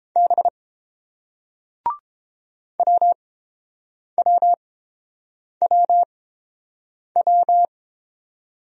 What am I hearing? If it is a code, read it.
Morse code, heard as a keyed tone, about 30 wpm, spaced as for 7 wpm, 700 Hz. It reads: B WWWW